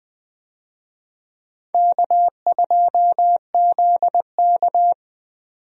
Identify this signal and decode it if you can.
Morse code, heard as a keyed tone, about 20 wpm, 700 Hz. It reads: K2ZK
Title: USA Call Signs - 4 Characters - 20wpm